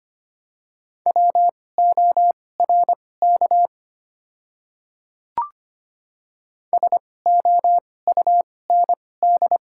Morse code, heard as a keyed tone, about 25 wpm, spaced as for 18 wpm, 690 Hz. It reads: WORK SOUND
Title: 2x Character Spacing - Top 200 Words - 25wpm